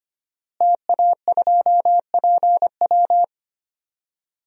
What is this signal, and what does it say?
Morse code, heard as a keyed tone, about 25 wpm, 700 Hz. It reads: TA2PW
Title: DXCC-100 Call Signs - 5 Characters - Repeated 2x - 25wpm